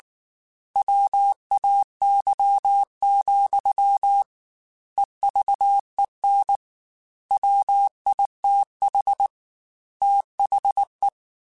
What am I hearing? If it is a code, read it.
Morse code, heard as a keyed tone, about 19 wpm, 800 Hz. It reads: WAY, EVEN WITH THE
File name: Technical Subjects / Technical Subjects-mp3s / OR Gate @18